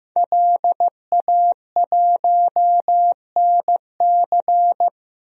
Morse code, heard as a keyed tone, about 15 wpm, 700 Hz. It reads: LA1NC
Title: DXCC-100 Call Signs - 5 Characters - Repeated 2x - 15wpm